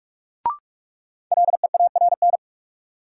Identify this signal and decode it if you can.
Morse code, heard as a keyed tone, about 46 wpm, 700 Hz. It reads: LEARN
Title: Sentences from Top 300 Words - New Word Drill - 45wpm